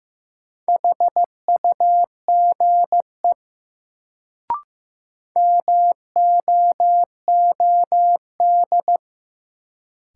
Morse code, heard as a keyed tone, about 15 wpm, 700 Hz. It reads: HUGE MOOD
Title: Sentences from Top 1100 Words - New Word Drill - 15wpm